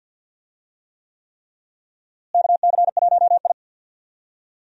Morse code, heard as a keyed tone, about 52 wpm, 700 Hz. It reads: KC1I
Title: USA Call Signs - 4 Characters - Repeated 3x - 50wpm